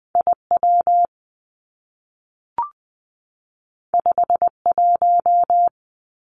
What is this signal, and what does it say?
Morse code, heard as a keyed tone, about 20 wpm, 700 Hz. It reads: I W 5 1
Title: Two Challenging Letter-Number Combinations 20wpm